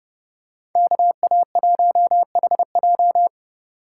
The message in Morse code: KA1HJ

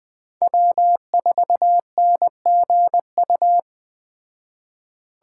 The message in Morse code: W4NGU

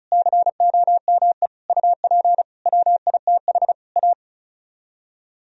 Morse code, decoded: COME UP WITH A

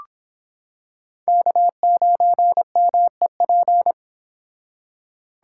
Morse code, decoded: K9MEP

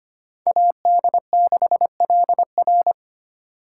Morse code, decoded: AD6LR

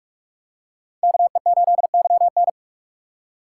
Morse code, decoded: KE8YN

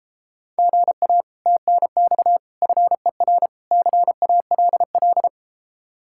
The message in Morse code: GA TNX FER CALL

